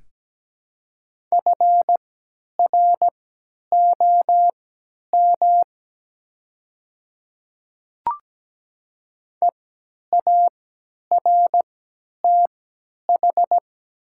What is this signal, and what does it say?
Morse code, heard as a keyed tone, about 17 wpm, spaced as for 10 wpm, 700 Hz. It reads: FROM EARTH
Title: 3x Character Spacing - Top 300 Words - 17wpm